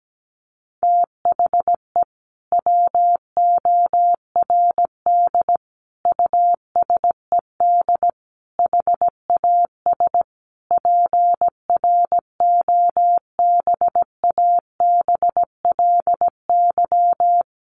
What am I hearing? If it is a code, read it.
Morse code, heard as a keyed tone, about 17 wpm, 700 Hz. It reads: THE WORD USED HAS PROBABLY
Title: Sentences from Top 1100 Words 17wpm